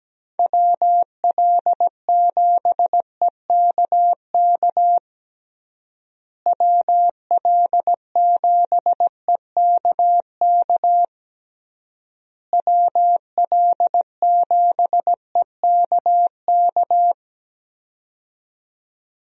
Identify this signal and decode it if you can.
Morse code, heard as a keyed tone, about 17 wpm, 700 Hz. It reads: WL7EKK WL7EKK WL7EKK
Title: USA Call Signs - 6 Characters - Repeated 3x - 17wpm